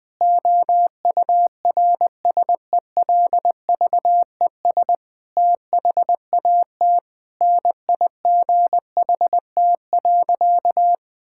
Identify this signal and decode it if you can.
Morse code, heard as a keyed tone, about 20 wpm, 700 Hz. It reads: OURSELVES THAT NIGHT.
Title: Sentences from Top 1600 Words 20wpm